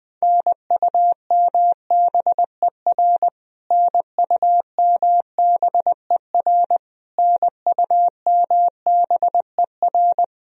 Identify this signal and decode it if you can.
Morse code, heard as a keyed tone, about 20 wpm, 700 Hz. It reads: NUMBER NUMBER NUMBER
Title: Top 100 Words Repeated 20wpm